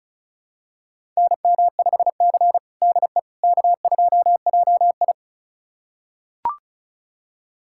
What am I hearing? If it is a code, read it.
Morse code, heard as a keyed tone, about 36 wpm, 690 Hz. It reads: NM5C DE K2JI
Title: USA Callsign DE Callsign - 4 Characters - 35wpm